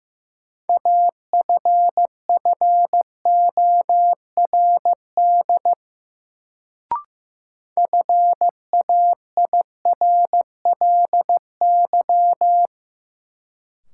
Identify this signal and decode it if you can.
Morse code, heard as a keyed tone, about 15 wpm, 700 Hz. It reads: AFFORD FAIRLY
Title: Sentences from Top 1100 Words - New Word Drill Alt - 15wpm